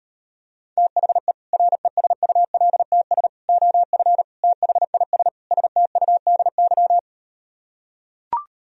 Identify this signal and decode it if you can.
Morse code, heard as a keyed tone, about 38 wpm, 700 Hz. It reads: THE RESULTS OF THIS STUDY